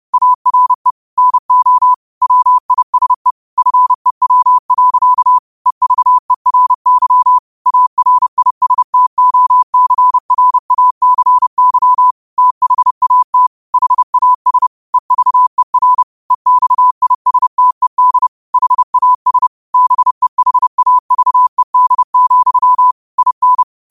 Morse code, heard as a keyed tone, about 30 wpm, 1000 Hz. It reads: ARE NO WISE FEW. EVERY ARISTOCRACY THAT HAS EVER EXISTED HAS BEHAVED, IN
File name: qotdfiles / 30wpm